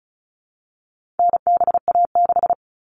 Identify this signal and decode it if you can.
Morse code, heard as a keyed tone, about 34 wpm, 700 Hz. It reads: NBA6
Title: All Cut Numbers - 4-Digit 35wpm